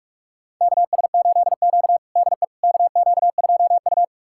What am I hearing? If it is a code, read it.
Morse code, heard as a keyed tone, about 45 wpm, 700 Hz. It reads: KS8Q DE KQ2U
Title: USA Callsign DE Callsign - 4 Characters - 45wpm